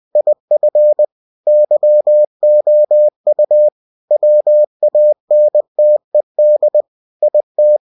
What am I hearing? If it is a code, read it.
Morse code, heard as a keyed tone, about 20 wpm, 600 Hz. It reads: IF YOU WANTED IT
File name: cw-books / winnie-the-pooh / winnie-20wpm-page-0110